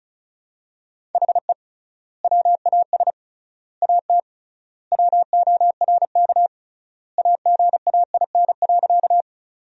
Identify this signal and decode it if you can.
Morse code, heard as a keyed tone, about 36 wpm, 700 Hz. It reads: HE WAS AT WORK AGAIN.
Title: 3x Word-Spacing - Sentences from Top 200 Words - 35wpm